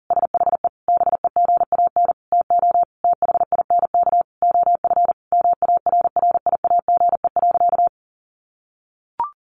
Morse code, heard as a keyed tone, about 40 wpm, 700 Hz. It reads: SHE BEGAN TO THINK OF MARRIAGE.
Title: Sentences from Top 800 Words 40wpm